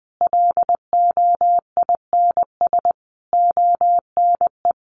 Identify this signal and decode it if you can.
Morse code, heard as a keyed tone, about 20 wpm, 700 Hz. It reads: LOINS ONE
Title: A Sound Of Thunder 20wpm